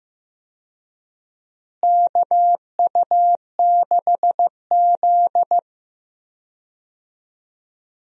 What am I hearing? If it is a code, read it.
Morse code, heard as a keyed tone, about 15 wpm, 700 Hz. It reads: KU6Z
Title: USA Call Signs - 4 Characters - Repeated 3x - 15wpm